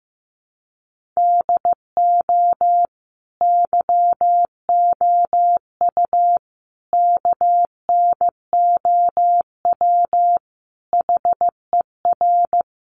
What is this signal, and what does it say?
Morse code, heard as a keyed tone, about 15 wpm, 700 Hz. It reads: DO YOU KNOW HER
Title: Sentences from Top 1400 Words 15wpm